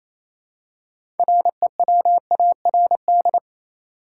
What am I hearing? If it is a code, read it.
Morse code, heard as a keyed tone, about 27 wpm, 700 Hz. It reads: REWARD
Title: Six Letter Words - 28wpm